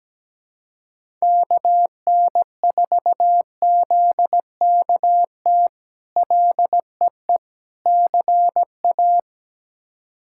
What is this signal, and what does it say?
Morse code, heard as a keyed tone, about 17 wpm, 700 Hz. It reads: KN4ZKT LEE CA
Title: K1USN SST QSO Practice - 17wpm